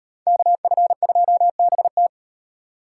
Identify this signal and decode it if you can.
Morse code, heard as a keyed tone, about 36 wpm, 700 Hz. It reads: KF2BT